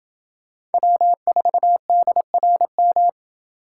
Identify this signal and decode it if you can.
Morse code, heard as a keyed tone, about 29 wpm, 700 Hz. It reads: W4DRM